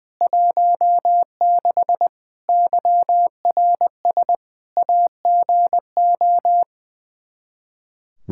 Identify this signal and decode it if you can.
Morse code, heard as a keyed tone, about 20 wpm, 700 Hz. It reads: 16 YRS AGO
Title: Practice QSO - Exchange 3 - More Info - 20wpm